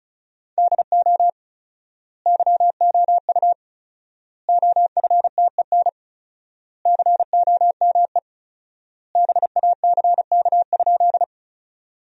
Morse code, heard as a keyed tone, about 34 wpm, 700 Hz. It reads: DO YOU OFTEN COME BACK?
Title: 4x Word-Spacing - Sentences from Top 300 Words - 35wpm